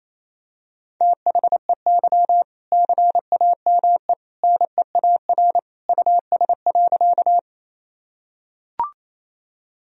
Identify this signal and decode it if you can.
Morse code, heard as a keyed tone, about 28 wpm, 700 Hz. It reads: THEY CAME NEAR US.
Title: Sentences from Top 1000 Words - Review Part 1 - 28wpm